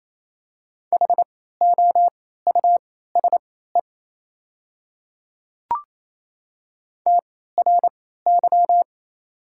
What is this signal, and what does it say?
Morse code, heard as a keyed tone, about 28 wpm, spaced as for 16 wpm, 700 Hz. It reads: HOUSE TRY
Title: 3x Character Spacing - Top 200 Words - 28wpm